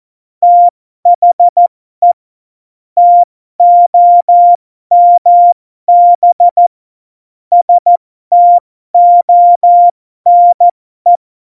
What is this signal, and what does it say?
Morse code, heard as a keyed tone, about 14 wpm, spaced as for 12 wpm, 700 Hz. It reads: THE TOMB STONE
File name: Morsekurs / Witze